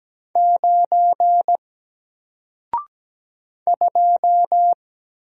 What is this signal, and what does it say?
Morse code, heard as a keyed tone, about 17 wpm, 700 Hz. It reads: 9 2
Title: Single Letter-Number 17wpm